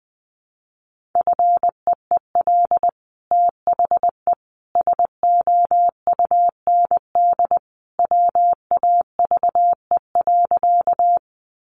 Morse code, heard as a keyed tone, about 19 wpm, 700 Hz. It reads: FEEL THE SOUND WAVE.
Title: Sentences from Top 1400 Words 20wpm